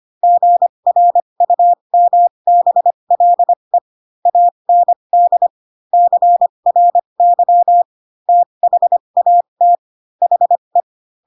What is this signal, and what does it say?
Morse code, heard as a keyed tone, about 25 wpm, spaced as for 22 wpm, 700 Hz. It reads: GRUMBLE AND CRY THAT HE
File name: Pinocchio / Pinocchio ch8 18 25 ik1hge